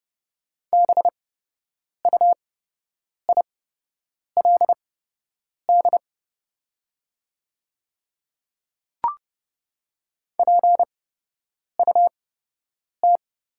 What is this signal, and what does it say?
Morse code, heard as a keyed tone, about 30 wpm, spaced as for 8 wpm, 700 Hz. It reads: BUILD PUT
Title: 8x Character Spacing - Top 200 Words - 30wpm